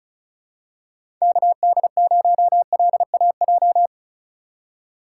Morse code, heard as a keyed tone, about 36 wpm, 700 Hz. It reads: KD0LAJ
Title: USA Call Signs - 4x Character Spacing - 35wpm